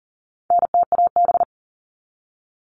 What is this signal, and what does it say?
Morse code, heard as a keyed tone, about 40 wpm, 700 Hz. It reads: NTAB
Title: All Cut Numbers - 4-Digit 40wpm